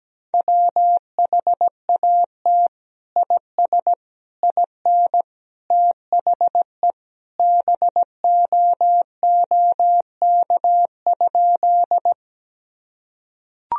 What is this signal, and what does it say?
Morse code, heard as a keyed tone, about 17 wpm, 700 Hz. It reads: WHAT IS IN THE BOOK?
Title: Sentences from Top 1000 Words - Review Part 1 - 17wpm